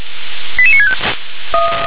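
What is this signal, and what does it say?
Five-tone selective call (ZVEI), heard as quick runs of five tones, tones of about 70 ms. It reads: 88965
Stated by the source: ZVEI five-tone selective call received on SDR.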